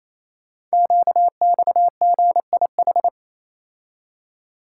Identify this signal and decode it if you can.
Morse code, heard as a keyed tone, about 27 wpm, 700 Hz. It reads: QXGIH
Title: Code Groups - 5 Letters - 28wpm